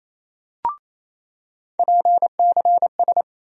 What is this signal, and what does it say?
Morse code, heard as a keyed tone, about 28 wpm, 700 Hz. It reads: PCS
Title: USA Call Sign Postfixes - 3 Letters 28wpm